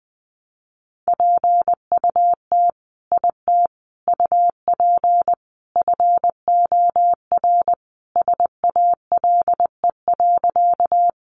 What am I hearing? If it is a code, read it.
Morse code, heard as a keyed tone, about 20 wpm, 700 Hz. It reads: PUT IT UP FOR SALE.
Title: Sentences from Top 1400 Words 20wpm